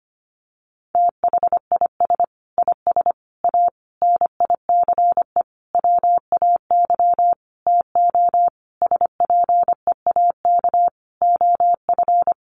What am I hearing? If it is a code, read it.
Morse code, heard as a keyed tone, about 25 wpm, 700 Hz. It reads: THIS IS A NICE WAY TO SPEAK OF
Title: Sentences from Top 1100 Words 25wpm